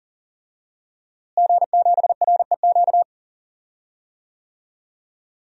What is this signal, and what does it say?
Morse code, heard as a keyed tone, about 40 wpm, 700 Hz. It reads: G7REQ
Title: DXCC-100 Call Signs - 5 Characters - Repeated 3x - 40wpm